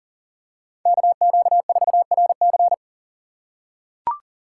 Morse code, heard as a keyed tone, about 40 wpm, 700 Hz. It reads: KQ4RC
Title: USA Call Signs - 5 Characters - 2x Character Spacing - 40wpm